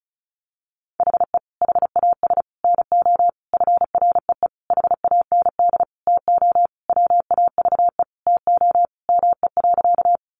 Morse code, heard as a keyed tone, about 34 wpm, 700 Hz. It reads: HE HAS NO FREE HAND TO WAVE TO ME.
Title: Sentences from Top 1400 Words 35wpm